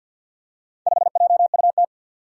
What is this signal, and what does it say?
Morse code, heard as a keyed tone, about 52 wpm, 700 Hz. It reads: 5JUT
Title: Four Challenging Letter-Number Combinations 50wpm